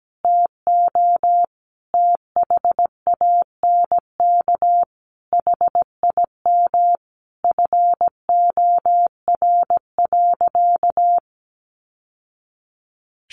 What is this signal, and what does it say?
Morse code, heard as a keyed tone, about 17 wpm, 700 Hz. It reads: TO THANK HIM FOR.
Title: Sentences from Top 1000 Words 17wpm